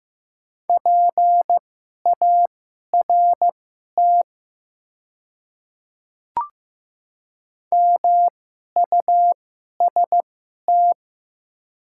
Morse code, heard as a keyed tone, about 15 wpm, spaced as for 11 wpm, 700 Hz. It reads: PART MUST
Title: 2x Character Spacing - Top 200 Words - 15wpm